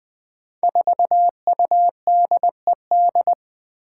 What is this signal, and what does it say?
Morse code, heard as a keyed tone, about 20 wpm, 700 Hz. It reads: 4UDED
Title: Five Challenging Letter-Number Combinations 20wpm